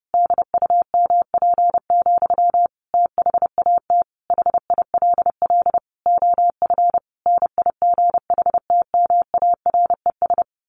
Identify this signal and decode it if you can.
Morse code, heard as a keyed tone, about 30 wpm, 700 Hz. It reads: DUMP, THAT HILL OF NIGHTMARES
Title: A Sound Of Thunder 30wpm